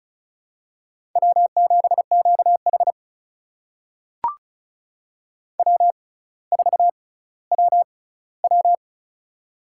Morse code, heard as a keyed tone, about 36 wpm, spaced as for 18 wpm, 700 Hz. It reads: W7QH W4WW
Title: USA Call Signs - 4 Characters - 6x Character Spacing - 35wpm